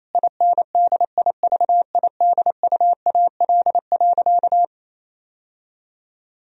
Morse code, heard as a keyed tone, about 28 wpm, 700 Hz. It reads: INDIVIDUAL.
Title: Sentences from Top 1000 Words - Review Part 4 - 28wpm